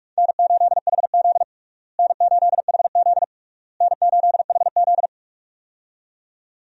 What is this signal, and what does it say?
Morse code, heard as a keyed tone, about 47 wpm, 700 Hz. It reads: N8HZ N8HZ N8HZ
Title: USA Call Signs - 4 Characters - Repeated 3x - 45wpm